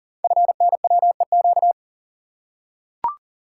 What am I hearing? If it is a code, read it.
Morse code, heard as a keyed tone, about 38 wpm, 690 Hz. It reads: FNWEQ
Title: Code Groups - 5 Letters - 40wpm